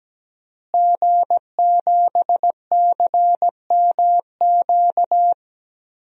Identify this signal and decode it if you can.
Morse code, heard as a keyed tone, about 17 wpm, 700 Hz. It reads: G7CMQ
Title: DXCC-100 Call Signs - 5 Characters - Repeated 2x - 17wpm